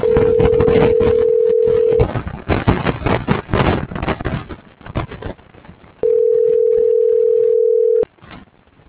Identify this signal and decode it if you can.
Ringback tone (North America), 440 + 480 Hz.
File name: CENTRAL BLN / out ericka.canales - .71435